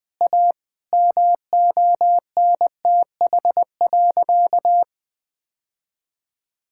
Morse code, heard as a keyed tone, about 21 wpm, 700 Hz. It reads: A MONTH.
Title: Sentences from Top 1000 Words - Review Part 5 - 20wpm